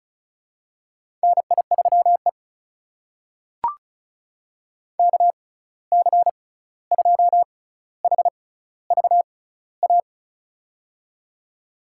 Morse code, heard as a keyed tone, about 34 wpm, spaced as for 16 wpm, 700 Hz. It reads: NI3E KC2HVA
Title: USA Call Signs - 6x Character Spacing - 35wpm